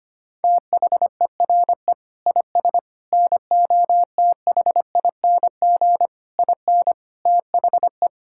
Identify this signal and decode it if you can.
Morse code, heard as a keyed tone, about 25 wpm, 700 Hz. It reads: THERE IS NOTHING IN THE
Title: Sentences from Top 1600 Words 25wpm